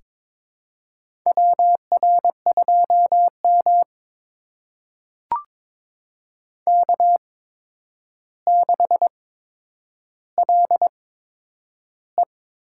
Morse code, heard as a keyed tone, about 21 wpm, spaced as for 9 wpm, 700 Hz. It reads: WR2M K6LE
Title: USA Call Signs - 4 Characters - 8x Character Spacing - 22wpm